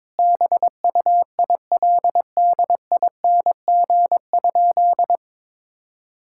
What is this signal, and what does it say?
Morse code, heard as a keyed tone, about 21 wpm, 700 Hz. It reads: BUILDING?
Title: Sentences from Top 1000 Words - Review Part 5 - 22wpm